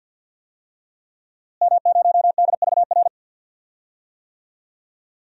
Morse code, heard as a keyed tone, about 51 wpm, 700 Hz. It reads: M0DVR
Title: DXCC-100 Call Signs - 5 Characters - Repeated 2x - 50wpm